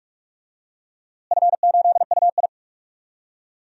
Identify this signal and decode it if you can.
Morse code, heard as a keyed tone, about 44 wpm, 700 Hz. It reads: F8UI
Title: DXCC-100 Call Signs - 4 Characters - Repeated 2x - 45wpm